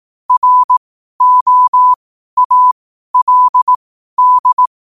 Morse code, heard as a keyed tone, about 18 wpm, spaced as for 12 wpm, 1000 Hz. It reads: ROALD
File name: qotdfiles / 12wpm